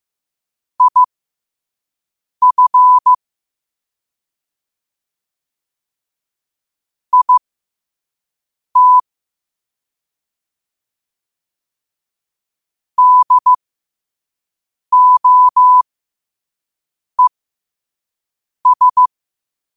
Morse code, heard as a keyed tone, about 15 wpm, spaced as for 5 wpm, 1000 Hz. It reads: IF IT DOES
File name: qotdfiles / 5wpm